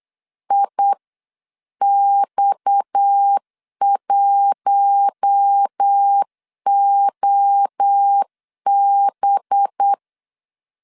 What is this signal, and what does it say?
Morse code, heard as a keyed tone, about 8 wpm, 800 Hz. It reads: IX1OB
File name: CW / DET / 1 6 1500C S B 8wpm 5142